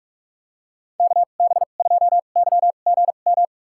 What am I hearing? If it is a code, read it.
Morse code, heard as a keyed tone, about 46 wpm, spaced as for 33 wpm, 700 Hz. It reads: KB2YGM
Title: USA Call Signs - 6 Characters - 2x Character Spacing - 45wpm